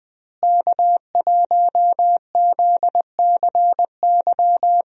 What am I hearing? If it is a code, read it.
Morse code, heard as a keyed tone, about 20 wpm, 700 Hz. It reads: K1ZCY
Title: USA Call Signs - 5 Characters - Repeated 3x - 20wpm